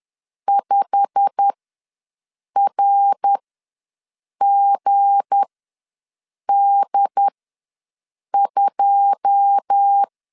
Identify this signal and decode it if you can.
Morse code, heard as a keyed tone, about 10 wpm, spaced as for 6 wpm, 800 Hz. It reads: 5RGD2
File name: LEZIONI / 9 2 ws 10wpm 4130